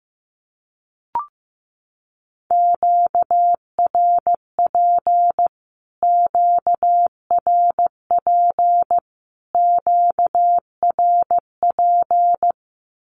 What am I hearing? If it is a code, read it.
Morse code, heard as a keyed tone, about 15 wpm, 700 Hz. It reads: QRP QRP QRP
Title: Q-Codes Repeated 3x 15wpm